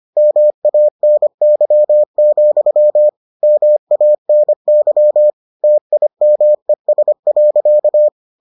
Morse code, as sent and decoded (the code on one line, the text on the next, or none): -- .- -. -.-- --..-- / -- .- -. -.-- / - .. -- . ... .-.-.-
MANY, MANY TIMES.